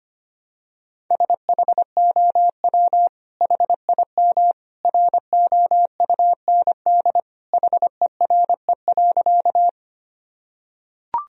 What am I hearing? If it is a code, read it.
Morse code, heard as a keyed tone, about 25 wpm, 700 Hz. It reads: SHOW HIM ROUND HERE.
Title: Sentences from Top 1000 Words - Review Part 1 - 25wpm